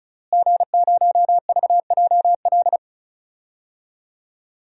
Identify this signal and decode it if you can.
Morse code, heard as a keyed tone, about 34 wpm, 700 Hz. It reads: G0VJL
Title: DXCC-100 Call Signs - 5 Characters - Repeated 3x - 35wpm